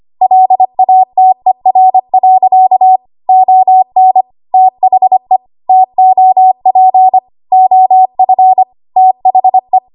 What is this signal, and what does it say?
Morse code, heard as a keyed tone, about 25 wpm, 750 Hz. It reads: LATER. ON THE TOP OF THE
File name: Archive / 25 WPM / 25WPM